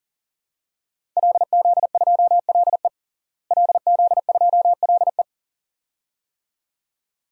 Morse code, heard as a keyed tone, about 40 wpm, 700 Hz. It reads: LZ2LE LZ2LE
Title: DXCC-100 Call Signs - 5 Characters - Repeated 2x - 40wpm